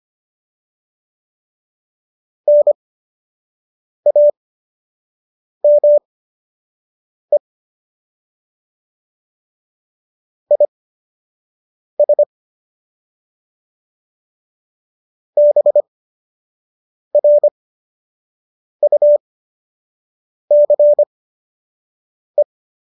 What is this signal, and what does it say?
Morse code, heard as a keyed tone, about 25 wpm, spaced as for 6 wpm, 590 Hz. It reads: NAME IS BRUCE